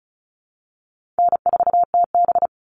Morse code, heard as a keyed tone, about 34 wpm, 700 Hz. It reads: N4TB
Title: Easy Call Signs Repeated 3x - 35wpm